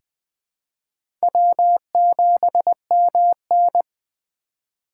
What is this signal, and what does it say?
Morse code, heard as a keyed tone, about 21 wpm, 700 Hz. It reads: W7MN
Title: USA Call Signs - 4 Characters - 3x Character Spacing - 20wpm